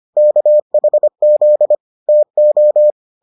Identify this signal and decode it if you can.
Morse code, heard as a keyed tone, about 25 wpm, 600 Hz. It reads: KHZ TO